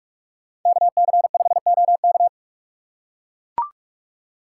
Morse code, heard as a keyed tone, about 41 wpm, 700 Hz. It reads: KC5OK